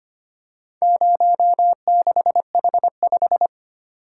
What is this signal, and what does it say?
Morse code, heard as a keyed tone, about 25 wpm, 700 Hz. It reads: 06H5
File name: practice / 25wpm / Four Letter-Number Combinations - 25wpm